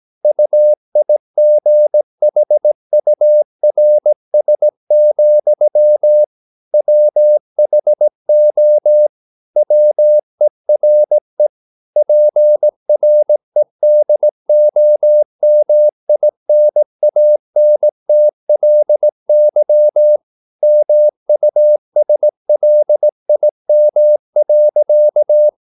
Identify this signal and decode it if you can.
Morse code, heard as a keyed tone, about 18 wpm, 600 Hz. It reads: UIGHURS, WHO WERE PREDOMINANTLY MUSLIM.
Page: Romanen Never af Ken Follet på engelsk.